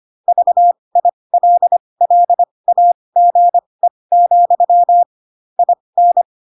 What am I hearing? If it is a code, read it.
Morse code, heard as a keyed tone, about 25 wpm, spaced as for 20 wpm, 700 Hz. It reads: VILLAGE, IN